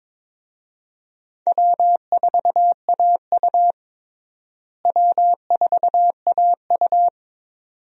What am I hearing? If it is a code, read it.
Morse code, heard as a keyed tone, about 22 wpm, 700 Hz. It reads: W4AU W4AU
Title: USA Call Signs - 4 Characters - Repeated 2x - 22wpm